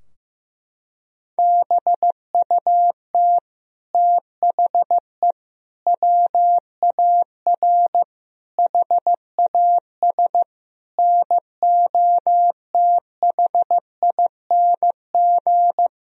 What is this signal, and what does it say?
Morse code, heard as a keyed tone, about 15 wpm, 700 Hz. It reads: BUT THE WAR HAS NOTHING
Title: Sentences from Top 1000 Words - Review Part 2 - 15wpm